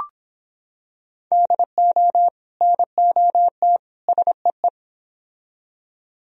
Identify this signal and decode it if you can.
Morse code, heard as a keyed tone, about 25 wpm, 700 Hz. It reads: DO NOT SEE